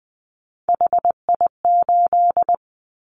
Morse code, heard as a keyed tone, about 20 wpm, 700 Hz. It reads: HI8
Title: Callsign Prefixes for Top 100 DXCC - 20wpm